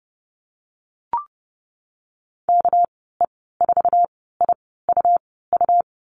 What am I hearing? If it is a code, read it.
Morse code, heard as a keyed tone, about 30 wpm, spaced as for 17 wpm, 700 Hz. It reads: KE4IUU
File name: Lesson 37g - Letter x - Call Signs 30wpm